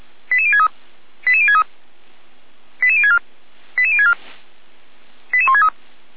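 Five-tone selective call (ZVEI), heard as quick runs of five tones, tones of about 70 ms. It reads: 88063 88063 88064 88064 88163
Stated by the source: ZVEI five-tone selective call received on an SDR, from the Feuerwehr Orlamünde audio server.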